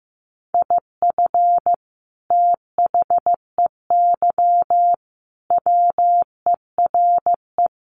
Morse code, heard as a keyed tone, about 15 wpm, 700 Hz. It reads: IF THEY WERE
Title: Sentences from Top 1200 Words 15wpm